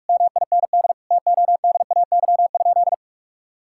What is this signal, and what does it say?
Morse code, heard as a keyed tone, about 41 wpm, 700 Hz. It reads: MIND TODAY?
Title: Sentences from Top 1000 Words - Review Part 6 - 45wpm